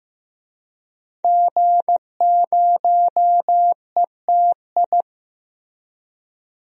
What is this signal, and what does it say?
Morse code, heard as a keyed tone, about 15 wpm, 700 Hz. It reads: G0ETI
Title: DXCC-100 Call Signs - 5 Characters - Repeated 3x - 15wpm